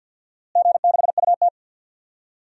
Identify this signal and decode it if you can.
Morse code, heard as a keyed tone, about 52 wpm, 700 Hz. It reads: G6UT
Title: DXCC-100 Call Signs - 4 Characters - Repeated 3x - 50wpm